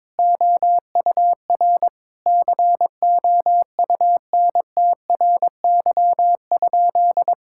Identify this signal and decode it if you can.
Morse code, heard as a keyed tone, about 21 wpm, 700 Hz. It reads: OUR COUNTRY?
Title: Sentences from Top 1000 Words - Review Part 5 - 22wpm